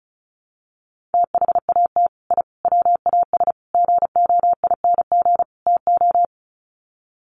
Morse code, heard as a keyed tone, about 34 wpm, 700 Hz. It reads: THAT I WAS GOING TO
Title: Sets of 5 Words Encore 35wpm